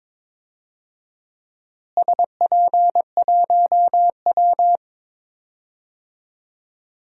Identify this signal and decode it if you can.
Morse code, heard as a keyed tone, about 22 wpm, 700 Hz. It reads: SP1W
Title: DXCC-100 Call Signs - 4 Characters - Repeated 3x - 22wpm